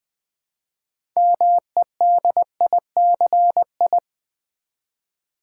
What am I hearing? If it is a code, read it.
Morse code, heard as a keyed tone, about 20 wpm, 700 Hz. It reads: MEDICI
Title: Build A Word - 2 Letters At A Time - 8-Letter Words - 20wpm